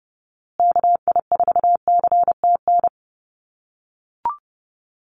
Morse code, heard as a keyed tone, about 29 wpm, 700 Hz. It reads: KI4CTN
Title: USA Call Signs Repeated 2x 30wpm